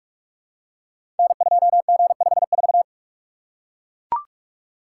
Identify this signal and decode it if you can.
Morse code, heard as a keyed tone, about 44 wpm, 700 Hz. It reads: N2G54
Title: Code Groups - 5 Letter:Numbers - 45wpm